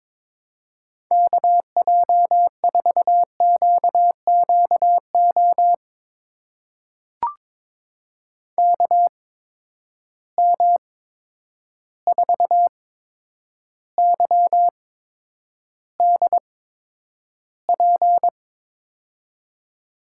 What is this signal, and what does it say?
Morse code, heard as a keyed tone, about 22 wpm, spaced as for 9 wpm, 700 Hz. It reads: KJ4QQO KM4YDP
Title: USA Call Signs - 6 Characters - 8x Character Spacing - 22wpm